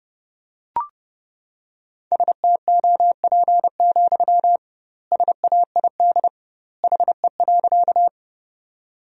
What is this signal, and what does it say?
Morse code, heard as a keyed tone, about 30 wpm, 700 Hz. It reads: STOP, SAID HE.
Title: 2x Word-Spacing - Sentences from Top 300 Words - 30wpm